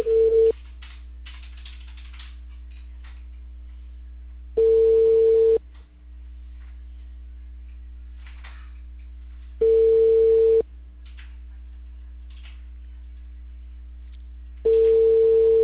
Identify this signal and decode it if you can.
Ringback tone (China, 450 Hz).